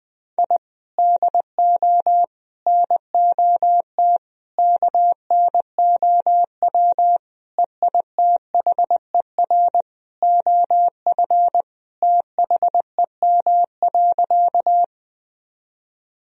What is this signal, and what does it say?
Morse code, heard as a keyed tone, about 21 wpm, 700 Hz. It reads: I DO NOT KNOW EITHER OF THEM.
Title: Sentences from Top 1000 Words - Review Part 4 - 20wpm